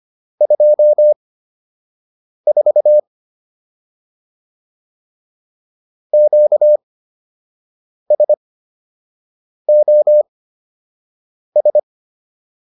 Morse code, heard as a keyed tone, about 25 wpm, spaced as for 6 wpm, 600 Hz. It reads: 24 QSOS